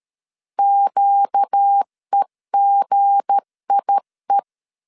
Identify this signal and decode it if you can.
Morse code, heard as a keyed tone, about 13 wpm, 800 Hz. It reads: QEGIE
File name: LEZIONI / 7 GF9Z / 7 G 12wpm 2055